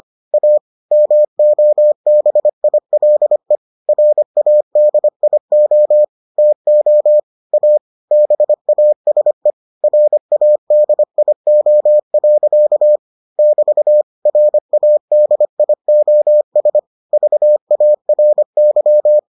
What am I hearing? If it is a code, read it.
Morse code, heard as a keyed tone, about 25 wpm, 600 Hz. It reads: A MOBILE RADIO TO A BASE RADIO. <BT> RADIOS VARY